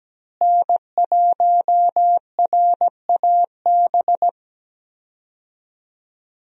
Morse code, heard as a keyed tone, about 17 wpm, 700 Hz. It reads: N1RAB
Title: USA Call Signs - 5 Characters - Repeated 2x - 17wpm